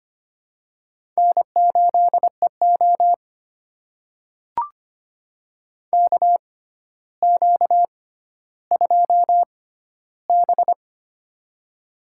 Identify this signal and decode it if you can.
Morse code, heard as a keyed tone, about 25 wpm, spaced as for 13 wpm, 700 Hz. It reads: N8EO KQ2B
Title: USA Call Signs - 4 Characters - 6x Character Spacing - 25wpm